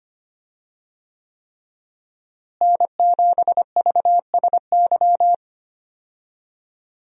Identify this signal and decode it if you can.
Morse code, heard as a keyed tone, about 25 wpm, 700 Hz. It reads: N7VSY
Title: USA Call Signs - 5 Characters - Repeated 3x - 25wpm